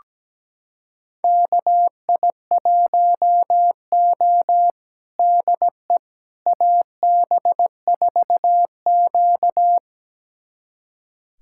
Morse code, heard as a keyed tone, about 17 wpm, 700 Hz. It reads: KI1O DE AB4Q
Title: USA Callsign DE Callsign - 4 Characters - 17wpm